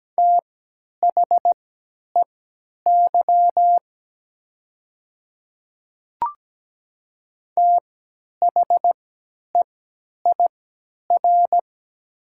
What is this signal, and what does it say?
Morse code, heard as a keyed tone, about 17 wpm, spaced as for 10 wpm, 690 Hz. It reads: THEY THEIR
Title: 3x Character Spacing - Top 100 Words - 17wpm